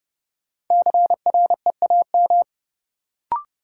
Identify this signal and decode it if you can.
Morse code, heard as a keyed tone, about 30 wpm, 700 Hz. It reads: CREAM
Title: Sentences from Top 1500 Words - New Word Drill - 30wpm